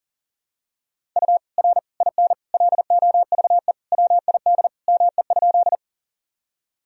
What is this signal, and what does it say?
Morse code, heard as a keyed tone, about 37 wpm, 700 Hz. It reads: U R IN LOVE WID ME?
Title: Sentences from Top 400 Words and Abbreviations 40wpm